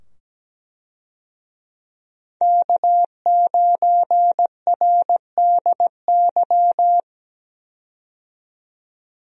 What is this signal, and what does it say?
Morse code, heard as a keyed tone, about 17 wpm, 700 Hz. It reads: K9RDY
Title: USA Call Signs - 5 Characters - 17wpm